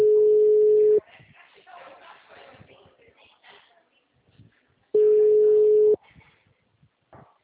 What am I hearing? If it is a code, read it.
Ringback tone (Europe, 425 Hz), with speech.
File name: out- -2002- - - .483